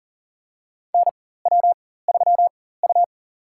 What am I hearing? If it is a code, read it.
Morse code, heard as a keyed tone, about 40 wpm, spaced as for 19 wpm, 700 Hz. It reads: NW3U
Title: USA Call Signs - 4 Characters - 4x Character Spacing - 40wpm